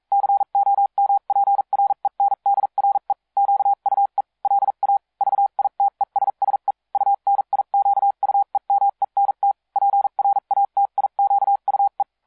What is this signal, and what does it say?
Morse code, heard as a keyed tone, about 42 wpm, spaced as for 36 wpm, 800 Hz. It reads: COMPRENDRE QUE LA VITESSE UNIQUEMENT PRATIQUE